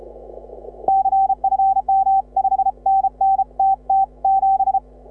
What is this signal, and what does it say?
Morse code, heard as a keyed tone, about 33 wpm, 780 Hz. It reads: CFM 5NN TT7
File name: AUDIO2018 / HA6OI CW